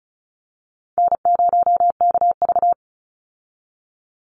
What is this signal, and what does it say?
Morse code, heard as a keyed tone, about 34 wpm, 700 Hz. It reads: N0KV
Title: Call Signs Repeated 3x - 35wpm